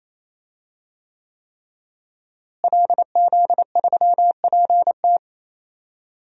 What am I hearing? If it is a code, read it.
Morse code, heard as a keyed tone, about 28 wpm, 700 Hz. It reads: LZ3PT
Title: DXCC-100 Call Signs - 5 Characters - Repeated 3x - 28wpm